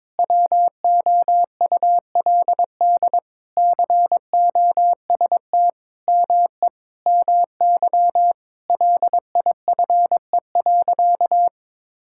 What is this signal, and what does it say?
Morse code, heard as a keyed tone, about 22 wpm, 700 Hz. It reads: WOULD COST ME MY LIFE.
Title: Sentences from Top 1000 Words - Review Part 5 - 22wpm